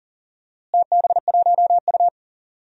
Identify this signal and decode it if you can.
Morse code, heard as a keyed tone, about 38 wpm, 700 Hz. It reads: TB1U
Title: DXCC-100 Call Signs - 4 Characters - Repeated 2x - 40wpm